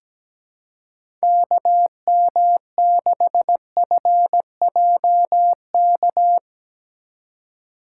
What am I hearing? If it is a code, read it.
Morse code, heard as a keyed tone, about 17 wpm, 700 Hz. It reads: KM6FJK
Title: USA Call Signs - 6 Characters - 2x Character Spacing - 17wpm